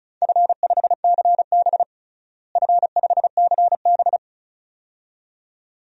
Morse code, heard as a keyed tone, about 36 wpm, 700 Hz. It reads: F5CB F5CB
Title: DXCC-100 Call Signs - 4 Characters - Repeated 2x - 35wpm